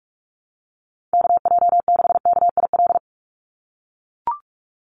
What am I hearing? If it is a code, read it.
Morse code, heard as a keyed tone, about 44 wpm, 700 Hz. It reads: KJ6KIL
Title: USA Call Signs 45wpm